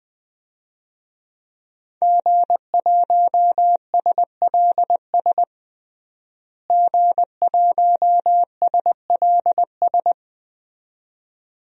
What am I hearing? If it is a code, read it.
Morse code, heard as a keyed tone, about 20 wpm, 700 Hz. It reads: G1SLS G1SLS
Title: DXCC-100 Call Signs - 5 Characters - Repeated 2x - 20wpm